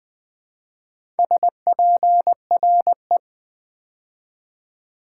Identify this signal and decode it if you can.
Morse code, heard as a keyed tone, about 20 wpm, 700 Hz. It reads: SPRE
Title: Build A Word - 2 Letters At A Time - 6-Letter Words - 20wpm